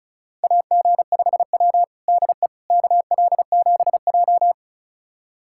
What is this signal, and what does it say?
Morse code, heard as a keyed tone, about 34 wpm, 700 Hz. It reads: AG5W DE KL7J
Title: USA Callsign DE Callsign - 4 Characters - 35wpm